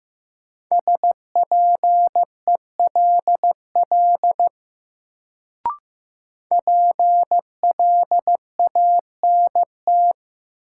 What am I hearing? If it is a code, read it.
Morse code, heard as a keyed tone, about 15 wpm, 700 Hz. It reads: SPELL PLANT
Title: Sentences from Top 1000 Words - Review Part 1 - New Word Drill - 15wpm